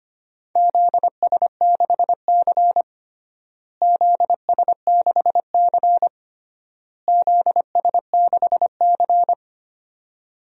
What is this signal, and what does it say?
Morse code, heard as a keyed tone, about 25 wpm, 700 Hz. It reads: ZS6C ZS6C ZS6C
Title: DXCC-100 Call Signs - 4 Characters - Repeated 3x - 25wpm